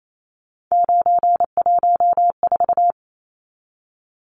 Morse code, heard as a keyed tone, about 28 wpm, 700 Hz. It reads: 914
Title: Numbers - 3-Digit 28wpm